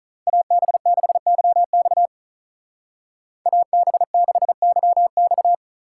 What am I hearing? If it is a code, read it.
Morse code, heard as a keyed tone, about 43 wpm, 700 Hz. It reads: AB6YX AB6YX
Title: Speed Racing - USA 2x2 Call Signs 35wpm